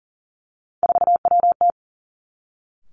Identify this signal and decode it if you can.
Morse code, heard as a keyed tone, about 38 wpm, 700 Hz. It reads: 4WT
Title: Three Challenging Letter-Number Combinations 40wpm